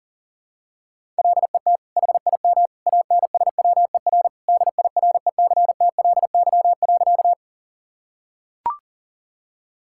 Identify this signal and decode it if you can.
Morse code, heard as a keyed tone, about 40 wpm, 700 Hz. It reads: LET HIM ANSWER DIRECTLY.
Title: Sentences from Top 1000 Words - Review Part 5 - 40wpm